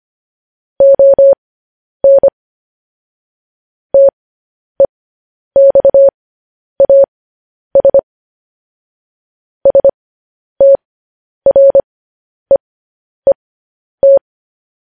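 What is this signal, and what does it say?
Morse code, heard as a keyed tone, about 25 wpm, spaced as for 10 wpm, 550 Hz. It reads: ON TEXAS STREET
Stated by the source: Morse code sending of news headlines, speed 10.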